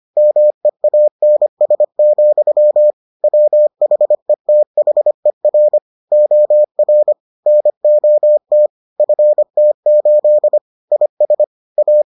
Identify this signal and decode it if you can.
Morse code, heard as a keyed tone, about 25 wpm, 600 Hz. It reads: MEANS, WHETHER OR NOT FT8 IS A